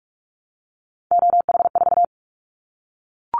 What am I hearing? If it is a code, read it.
Morse code, heard as a keyed tone, about 46 wpm, 700 Hz. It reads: OH4
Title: Callsign Prefixes for Top 25 DXCC - 45wpm